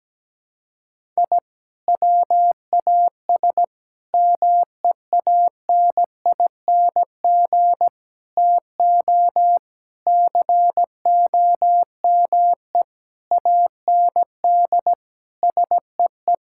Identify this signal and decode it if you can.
Morse code, heard as a keyed tone, about 17 wpm, 700 Hz. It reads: I WAS MEANING TO COME AND SEE
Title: Sentences from Top 1000 Words - Review Part 6 - 17wpm